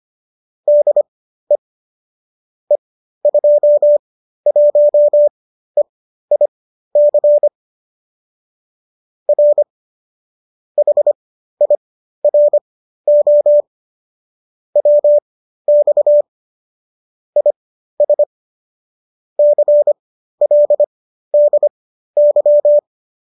Morse code, heard as a keyed tone, about 25 wpm, spaced as for 13 wpm, 590 Hz. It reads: DE E21EIC R HIRO WX IS CLDY